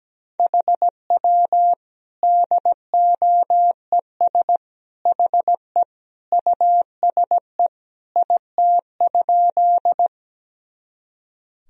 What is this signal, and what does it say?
Morse code, heard as a keyed tone, about 17 wpm, 700 Hz. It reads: HW DOES HE USE IT?
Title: Sentences from Top 200 Words and Abbreviations 17wpm